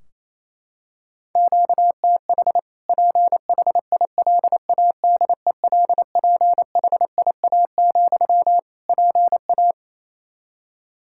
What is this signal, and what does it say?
Morse code, heard as a keyed tone, about 28 wpm, 700 Hz. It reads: QTH PHILADELPHIA, PA
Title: Practice QSO - Exchange 1 - Beginning - 28wpm